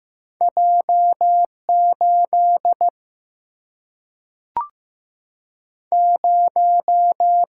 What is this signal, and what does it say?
Morse code, heard as a keyed tone, about 15 wpm, 700 Hz. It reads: J 8 0
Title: Single-Letter Single-Number 15wpm